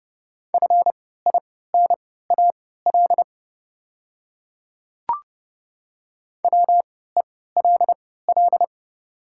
Morse code, heard as a keyed tone, about 30 wpm, spaced as for 17 wpm, 700 Hz. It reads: FINAL WELL
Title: 3x Character Spacing - Top 500 Words - 30wpm